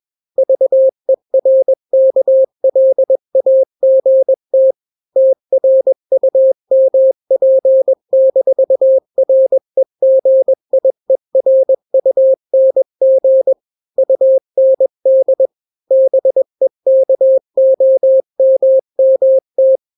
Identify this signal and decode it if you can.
Morse code, heard as a keyed tone, about 21 wpm, 530 Hz. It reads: VERKLAGT TRUMP-REGIERUNG UND BEKOMMT
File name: daily / 20wpm